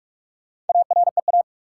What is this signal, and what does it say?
Morse code, heard as a keyed tone, about 46 wpm, 700 Hz. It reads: AREA
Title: Four Letter Words - 45wpm